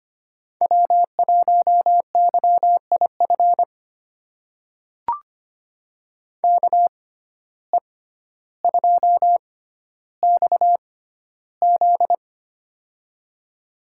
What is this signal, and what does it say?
Morse code, heard as a keyed tone, about 25 wpm, spaced as for 13 wpm, 700 Hz. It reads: W1YIF KE2XZ
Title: USA Call Signs - 6x Character Spacing - 25wpm